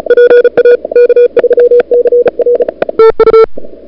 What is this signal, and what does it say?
Morse code, heard as a keyed tone, about 38 wpm, 440 Hz. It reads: PA K3CR TU
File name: CWSprint / February 2007 CW Sprint / 01-2007 CW Sprint Hour1